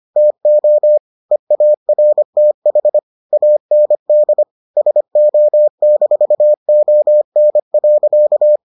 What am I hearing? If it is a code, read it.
Morse code, heard as a keyed tone, about 25 wpm, 600 Hz. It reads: TO EARTH AND SO-ON.